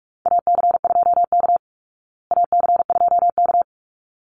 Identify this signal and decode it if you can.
Morse code, heard as a keyed tone, about 44 wpm, 700 Hz. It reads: AC2K AC2K
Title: Call Signs Repeated 3x - 45wpm